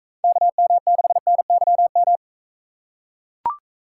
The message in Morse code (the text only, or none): KM6NYM